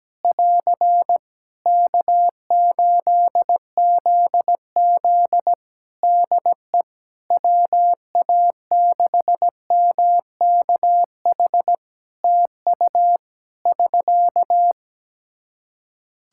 Morse code, heard as a keyed tone, about 17 wpm, 700 Hz. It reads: <AR> K8ZZ DE WA6MKH TU <SK>
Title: K1USN SST QSO Practice 17wpm